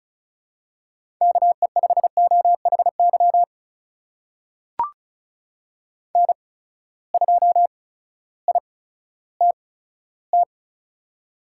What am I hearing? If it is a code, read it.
Morse code, heard as a keyed tone, about 36 wpm, spaced as for 16 wpm, 700 Hz. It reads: KE5OHY N2ITT
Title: USA Call Signs - 8x Character Spacing - 35wpm